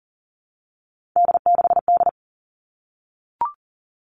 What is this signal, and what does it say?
Morse code, heard as a keyed tone, about 40 wpm, 700 Hz. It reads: D6D
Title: All Cut Numbers - 3-Digit 40wpm